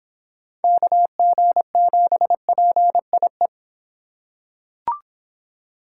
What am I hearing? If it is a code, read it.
Morse code, heard as a keyed tone, about 25 wpm, 700 Hz. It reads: KG7PIE